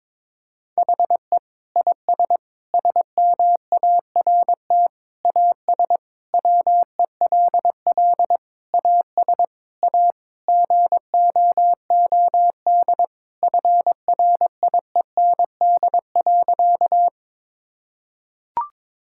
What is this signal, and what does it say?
Morse code, heard as a keyed tone, about 22 wpm, 700 Hz. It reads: HE IS SMART AS WELL AS A GOOD FRIEND.
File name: practice / 22wpm / Sentences with 4-Word Phrases - 22wpm